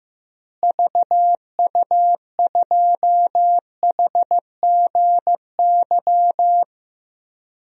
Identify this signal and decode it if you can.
Morse code, heard as a keyed tone, about 15 wpm, 700 Hz. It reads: VU2HGY
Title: DXCC-100 Call Signs - 6 Characters - Repeated 3x - 15wpm